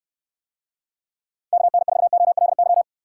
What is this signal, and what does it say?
Morse code, heard as a keyed tone, about 69 wpm, 700 Hz. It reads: KN4OLY